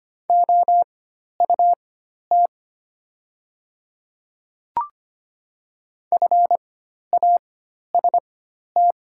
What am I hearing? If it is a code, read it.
Morse code, heard as a keyed tone, about 25 wpm, spaced as for 12 wpm, 700 Hz. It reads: OUT FAST
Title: 4x Character Spacing - Top 500 Words - 25wpm